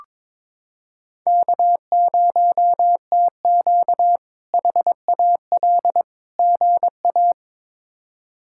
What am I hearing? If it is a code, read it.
Morse code, heard as a keyed tone, about 22 wpm, 700 Hz. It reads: K0TQ HAL GA
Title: K1USN SST QSO Practice - 22wpm